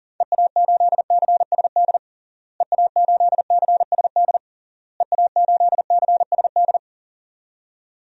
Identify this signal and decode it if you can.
Morse code, heard as a keyed tone, about 40 wpm, 700 Hz. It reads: EA8CSD EA8CSD EA8CSD
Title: DXCC-100 Call Signs - 6 Characters - Repeated 3x - 40wpm